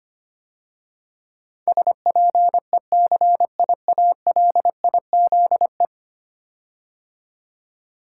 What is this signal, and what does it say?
Morse code, heard as a keyed tone, about 25 wpm, 700 Hz. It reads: SPECIALIZE
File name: 10 Letter Words - First 3 Letters Given - 25wpm